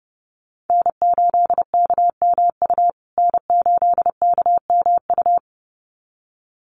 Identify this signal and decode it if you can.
Morse code, heard as a keyed tone, about 30 wpm, 700 Hz. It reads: N8KMU N8KMU
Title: USA Call Signs Repeated 2x 30wpm